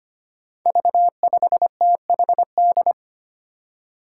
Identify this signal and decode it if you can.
Morse code, heard as a keyed tone, about 25 wpm, 700 Hz. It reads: V5THD
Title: Five Challenging Letter-Number Combinations 25wpm